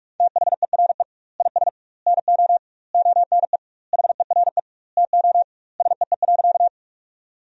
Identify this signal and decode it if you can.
Morse code, heard as a keyed tone, about 41 wpm, 700 Hz. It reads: THERE IS NO ONE HERE TO SEE.
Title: 2x Word-Spacing - Sentences from Top 200 Words - 45wpm